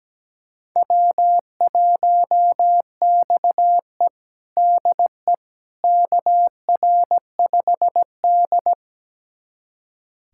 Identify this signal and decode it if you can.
Morse code, heard as a keyed tone, about 17 wpm, 700 Hz. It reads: W1XE DE KR5D
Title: USA Callsign DE Callsign - 4 Characters - 17wpm